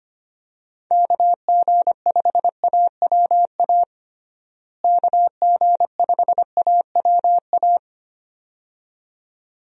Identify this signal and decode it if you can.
Morse code, heard as a keyed tone, about 25 wpm, 700 Hz. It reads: KG5AWA KG5AWA
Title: USA Call Signs - 6 Characters - Repeated 2x - 25wpm